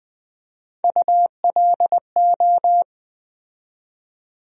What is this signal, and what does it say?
Morse code, heard as a keyed tone, about 20 wpm, 700 Hz. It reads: ULO
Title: USA Call Sign Postfixes - 3 Letters 20wpm